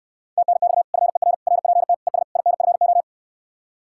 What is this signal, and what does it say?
Morse code, heard as a keyed tone, about 68 wpm, 700 Hz. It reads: TNX FER RPT ES INFO